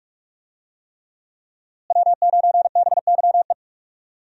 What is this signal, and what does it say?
Morse code, heard as a keyed tone, about 46 wpm, 700 Hz. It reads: W9BYE
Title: USA Call Signs - 5 Characters - Repeated 3x - 45wpm